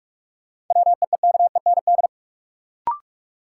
Morse code, heard as a keyed tone, about 46 wpm, 700 Hz. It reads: WEEKEND
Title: Sets of 5 Words - New Word Drill - 45wpm